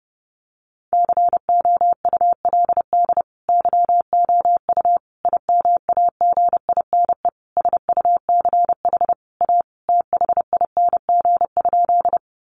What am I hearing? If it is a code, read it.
Morse code, heard as a keyed tone, about 30 wpm, 700 Hz. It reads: COULD YOU IMAGINE SUCH A THING?
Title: Sentences from Top 1400 Words 30wpm